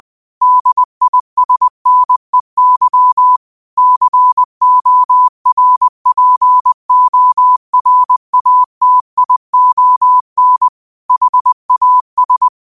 Morse code, heard as a keyed tone, about 20 wpm, 1000 Hz. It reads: DISNEY CORPORATION HAS
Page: New quotes every day in morse code at 20 Words per minute.